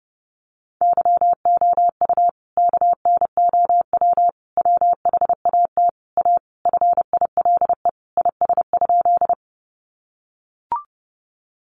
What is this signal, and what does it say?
Morse code, heard as a keyed tone, about 30 wpm, 700 Hz. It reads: YOU KNOW WHAT A FILE IS?
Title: Sentences from Top 1200 Words 30wpm